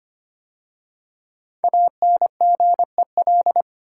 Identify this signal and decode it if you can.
Morse code, heard as a keyed tone, about 25 wpm, 700 Hz. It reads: ANGEL
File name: 5 Letter Words - First 3 Letters Given - 25wpm